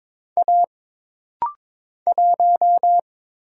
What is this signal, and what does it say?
Morse code, heard as a keyed tone, about 22 wpm, 700 Hz. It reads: A 1
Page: This rapid-fire timing is geared to push your ICR (Instant Character Recognition) skills.